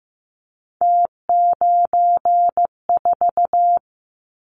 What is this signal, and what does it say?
Morse code, heard as a keyed tone, about 15 wpm, 700 Hz. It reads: T94
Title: Callsign Prefixes for Top 50 DXCC - 15wpm